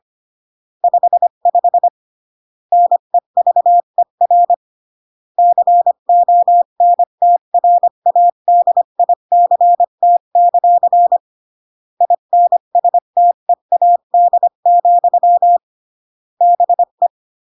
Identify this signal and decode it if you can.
Morse code, heard as a keyed tone, about 25 wpm, 700 Hz. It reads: 55 NEVER CONTRADICT; INSTEAD, BE
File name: CW / jodorowsky / 82 commandments 25-20